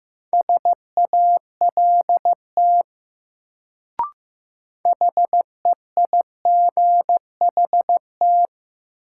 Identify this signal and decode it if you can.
Morse code, heard as a keyed tone, about 15 wpm, 700 Hz. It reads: SALT HEIGHT
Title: Sets of 5 Words - New Word Drill - 15wpm